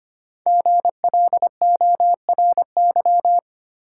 Morse code, heard as a keyed tone, about 25 wpm, 700 Hz. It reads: GLORY
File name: 5 Letter Words - First 3 Letters Given - 25wpm